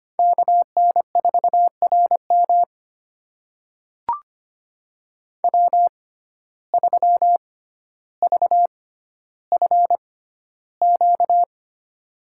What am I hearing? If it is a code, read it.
Morse code, heard as a keyed tone, about 25 wpm, spaced as for 13 wpm, 700 Hz. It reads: KN4RM W3VFQ
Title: USA Call Signs - 5 Characters - 6x Character Spacing - 25wpm